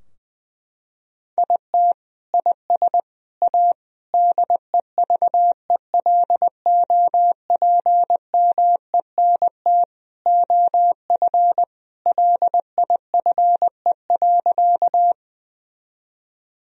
Morse code, heard as a keyed tone, about 20 wpm, 700 Hz. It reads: IT IS A DEVELOPMENT OF LIFE.
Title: Sentences from Top 1000 Words - Review Part 4 - 20wpm